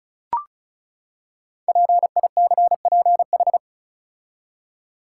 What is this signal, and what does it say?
Morse code, heard as a keyed tone, about 36 wpm, 690 Hz. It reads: PICPH
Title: Code Groups - 5 Letters - 35wpm